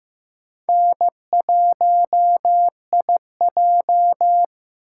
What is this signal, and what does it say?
Morse code, heard as a keyed tone, about 15 wpm, 700 Hz. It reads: N1IJ
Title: USA Call Signs - 4 Characters - 8x Character Spacing - 15wpm